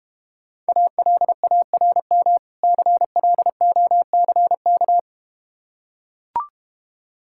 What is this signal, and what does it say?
Morse code, heard as a keyed tone, about 31 wpm, 700 Hz. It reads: ALARM CLOCK
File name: Easy Words - Speed-Racing with Hints - 25wpm